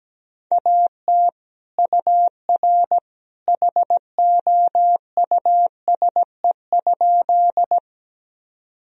Morse code, heard as a keyed tone, about 17 wpm, 700 Hz. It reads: AT UR HOUSE?
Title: Sentences from Top 200 Words and Abbreviations 17wpm